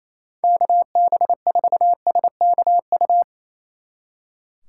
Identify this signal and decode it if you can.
Morse code, heard as a keyed tone, about 34 wpm, 700 Hz. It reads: KB4SKU
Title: Speed Racing - USA 2x3 Call Signs 28wpm